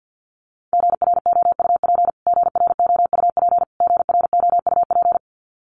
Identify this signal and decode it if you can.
Morse code, heard as a keyed tone, about 50 wpm, 700 Hz. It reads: GROUP GROUP GROUP
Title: Top 500 Words Repeated 50wpm